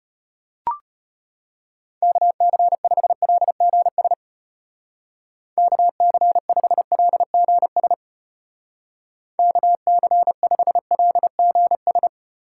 Morse code, heard as a keyed tone, about 34 wpm, 700 Hz. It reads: KC5LGS KC5LGS KC5LGS